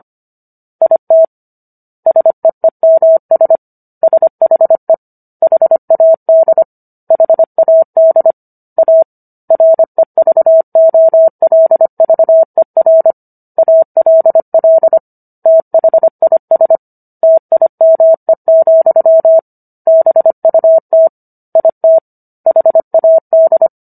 Morse code, heard as a keyed tone, about 25 wpm, 650 Hz. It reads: IT SEEMS SHE HAD HAD A REVOLVER ALL THIS TIME, BUT IT HAD